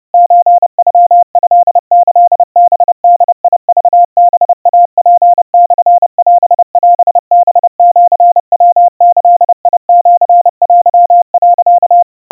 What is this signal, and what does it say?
Morse code (Japanese Wabun code), heard as a keyed tone, about 30 wpm, 700 Hz. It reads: ソノトキハボクハイツモオオハシヤギシテ、
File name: kawa / Morse sample / kamyu 110 cpm 700 Hz Crispy